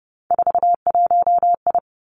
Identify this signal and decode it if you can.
Morse code, heard as a keyed tone, about 30 wpm, 700 Hz. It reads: 41I
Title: Three Challenging Letter-Number Combinations 30wpm